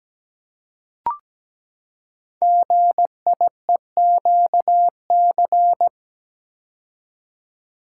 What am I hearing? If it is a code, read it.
Morse code, heard as a keyed tone, about 17 wpm, 690 Hz. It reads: GIEQC
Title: Code Groups - 5 Letter:Numbers - 17wpm